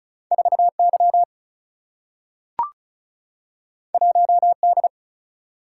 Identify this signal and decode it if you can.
Morse code, heard as a keyed tone, about 36 wpm, 700 Hz. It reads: 4Y 1D
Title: Single-Letter Single-Number 35wpm